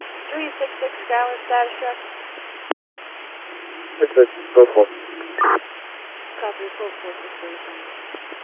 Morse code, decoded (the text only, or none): none